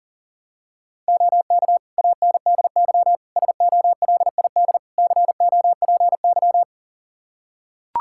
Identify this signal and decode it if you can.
Morse code, heard as a keyed tone, about 40 wpm, 700 Hz. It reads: OK ANDY SOLID COPY
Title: K1USN SST QSO Practice 40wpm